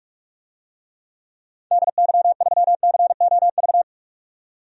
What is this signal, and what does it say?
Morse code, heard as a keyed tone, about 47 wpm, 700 Hz. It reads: DY3COV
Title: DXCC-100 Call Signs - 6 Characters - 45wpm